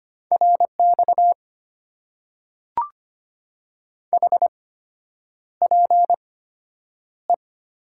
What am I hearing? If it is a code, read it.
Morse code, heard as a keyed tone, about 25 wpm, spaced as for 9 wpm, 700 Hz. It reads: RX HPE
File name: Top 90 QSO Elements - 8x Character Spacing - 25wpm